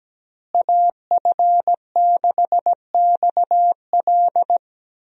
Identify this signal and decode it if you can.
Morse code, heard as a keyed tone, about 17 wpm, 700 Hz. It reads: AF6XL
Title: USA Call Signs - 5 Characters - Repeated 3x - 17wpm